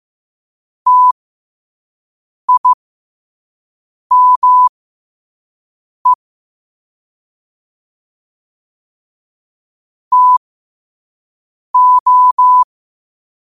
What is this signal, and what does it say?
Morse code, heard as a keyed tone, about 15 wpm, spaced as for 5 wpm, 1000 Hz. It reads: TIME TO
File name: qotdfiles / 5wpm